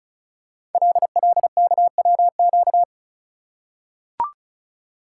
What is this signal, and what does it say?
Morse code, heard as a keyed tone, about 36 wpm, 690 Hz. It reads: LLKWQ
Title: Code Groups - 5 Letter:Numbers - 35wpm